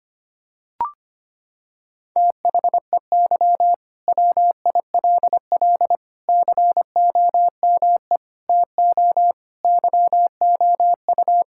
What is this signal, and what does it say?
Morse code, heard as a keyed tone, about 25 wpm, 700 Hz. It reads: THEY WILL COME TO YOU
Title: Sentences from Top 1600 Words 25wpm